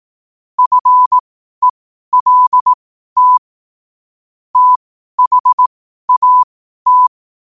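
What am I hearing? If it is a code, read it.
Morse code, heard as a keyed tone, about 18 wpm, spaced as for 12 wpm, 1000 Hz. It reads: FELT THAT
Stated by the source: Morse code sent at 12 words per minute.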